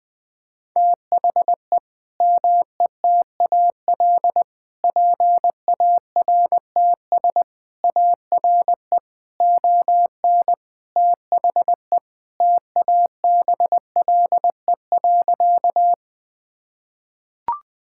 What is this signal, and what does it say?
Morse code, heard as a keyed tone, about 20 wpm, 700 Hz. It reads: THE METAL PARTS ARE ON THE TABLE.
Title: Sentences from Top 1000 Words - Review Part 6 - 20wpm